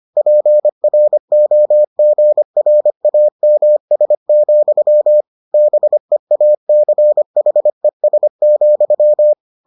Morse code, heard as a keyed tone, about 25 wpm, 600 Hz. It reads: PROGRAMS, BEACHES,